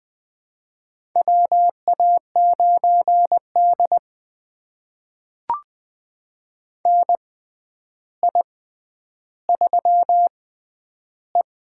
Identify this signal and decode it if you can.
Morse code, heard as a keyed tone, about 20 wpm, spaced as for 10 wpm, 700 Hz. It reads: WA9D NI3E
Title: USA Call Signs - 6x Character Spacing - 20wpm